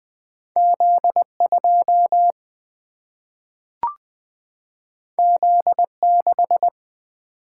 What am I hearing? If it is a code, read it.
Morse code, heard as a keyed tone, about 20 wpm, 700 Hz. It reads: Z 2 Z 6
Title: Single-Letter Single-Number 20wpm